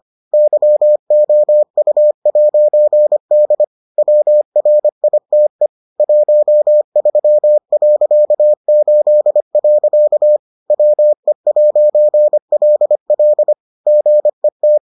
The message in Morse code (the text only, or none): YOU'D WRITE 13.8. WE'LL GET